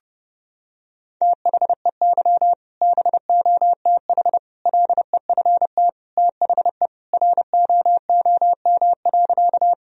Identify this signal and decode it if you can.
Morse code, heard as a keyed tone, about 30 wpm, 700 Hz. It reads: THEY BOTH LEFT THE ROOM.
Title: Sentences from Top 1000 Words - Review Part 1 - 30wpm